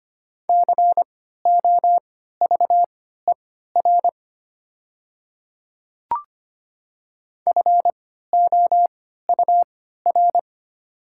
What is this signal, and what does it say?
Morse code, heard as a keyed tone, about 25 wpm, spaced as for 14 wpm, 700 Hz. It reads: COVER FOUR
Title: 3x Character Spacing - Top 300 Words - 25wpm